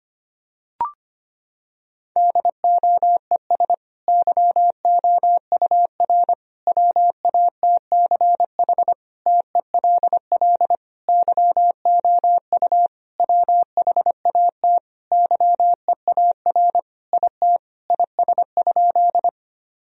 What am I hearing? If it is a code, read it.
Morse code, heard as a keyed tone, about 25 wpm, 700 Hz. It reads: DOES YOUR WATCH TELL YOU WHAT YEAR IT IS?
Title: Sentences from Top 1000 Words - Review Part 1 - 25wpm